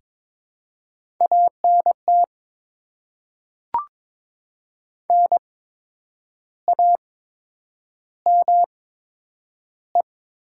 Morse code, heard as a keyed tone, about 22 wpm, spaced as for 8 wpm, 700 Hz. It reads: ANT NAME
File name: Top 20 QSO Elements - 8x Character Spacing - 22wpm